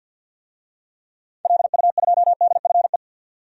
Morse code, heard as a keyed tone, about 49 wpm, 700 Hz. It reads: LU2DFE
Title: DXCC-100 Call Signs - 6 Characters - 50wpm